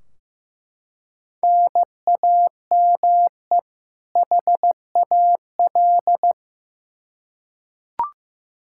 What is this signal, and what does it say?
Morse code, heard as a keyed tone, about 15 wpm, 700 Hz. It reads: NAME HAL
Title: Practice QSO - Exchange 1 - Beginning - 15wpm